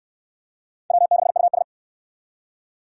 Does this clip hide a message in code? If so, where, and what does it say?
Yes: Morse code from 0.9 to 1.6 s, heard as a keyed tone, about 64 wpm, 700 Hz. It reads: W6US